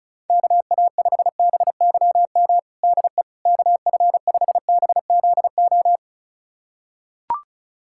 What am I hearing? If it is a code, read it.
Morse code, heard as a keyed tone, about 36 wpm, 700 Hz. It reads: KA5BYM DE KF5BZO
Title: USA Callsign DE Callsign 35wpm